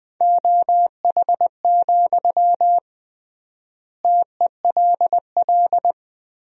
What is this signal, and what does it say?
Morse code, heard as a keyed tone, about 21 wpm, 700 Hz. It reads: OH, TELL
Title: 3x Word-Spacing - Sentences from Top 500 Words - 20wpm